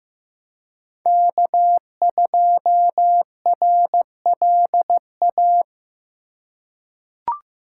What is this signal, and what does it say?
Morse code, heard as a keyed tone, about 15 wpm, 700 Hz. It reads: K2RLA
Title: Code Groups - 5 Letter:Numbers - 15wpm